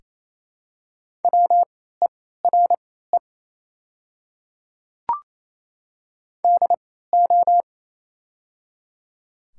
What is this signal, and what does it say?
Morse code, heard as a keyed tone, about 27 wpm, spaced as for 16 wpm, 700 Hz. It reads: WERE DO
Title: 3x Character Spacing - Top 300 Words - 28wpm